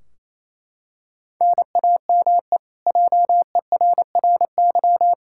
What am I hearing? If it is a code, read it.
Morse code, heard as a keyed tone, about 27 wpm, 700 Hz. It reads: NAME JERRY
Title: Practice QSO - Exchange 1 - Beginning - 28wpm